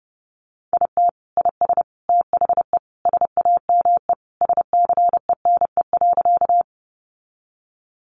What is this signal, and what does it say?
Morse code, heard as a keyed tone, about 30 wpm, 700 Hz. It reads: IT IS THE SAME SCENE.
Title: Sentences from Top 1100 Words 30wpm